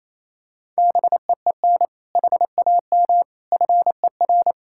Morse code, heard as a keyed tone, about 27 wpm, 700 Hz. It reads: BEEN HAM FER
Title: K1USN SST QSO Practice 28wpm